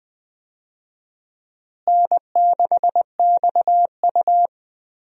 Morse code, heard as a keyed tone, about 20 wpm, 700 Hz. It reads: N6XU
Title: USA Call Signs - 4 Characters - Repeated 3x - 20wpm